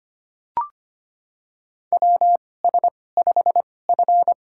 Morse code, heard as a keyed tone, about 25 wpm, spaced as for 18 wpm, 700 Hz. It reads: WS5F
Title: USA Call Signs - 4 Characters - 2x Character Spacing - 25wpm